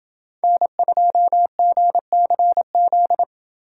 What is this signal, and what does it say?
Morse code, heard as a keyed tone, about 28 wpm, 700 Hz. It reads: N2GCZ